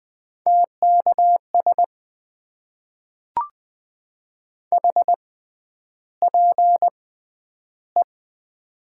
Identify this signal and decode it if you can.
Morse code, heard as a keyed tone, about 20 wpm, spaced as for 10 wpm, 700 Hz. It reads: TKS HPE
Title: Top 90 QSO Elements - 6x Character Spacing - 20wpm